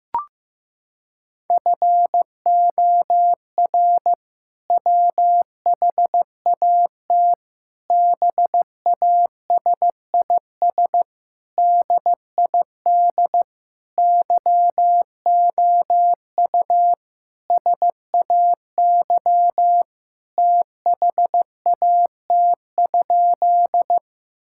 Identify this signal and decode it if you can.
Morse code, heard as a keyed tone, about 15 wpm, 700 Hz. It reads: FOR WHAT BASIS DID YOU SAY THAT?
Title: Sentences from Top 1000 Words - Review Part 5 - 15wpm